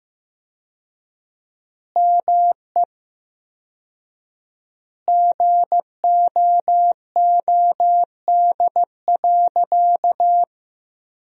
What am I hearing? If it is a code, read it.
Morse code, heard as a keyed tone, about 15 wpm, 700 Hz. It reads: ME GOOD.
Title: 4x Word-Spacing - Sentences from Top 200 Words - 15wpm